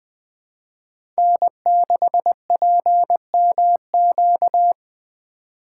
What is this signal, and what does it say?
Morse code, heard as a keyed tone, about 20 wpm, 700 Hz. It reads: N6PMQ
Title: USA Call Signs - 3x Character Spacing - 20wpm